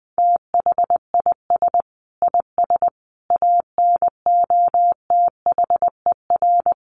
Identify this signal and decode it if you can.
Morse code, heard as a keyed tone, about 20 wpm, 700 Hz. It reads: THIS IS ANOTHER
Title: Sentences from Top 1400 Words 20wpm